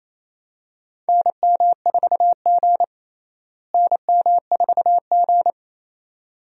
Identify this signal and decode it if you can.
Morse code, heard as a keyed tone, about 28 wpm, 700 Hz. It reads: NM4G NM4G
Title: USA Call Signs - 4 Characters - Repeated 2x - 28wpm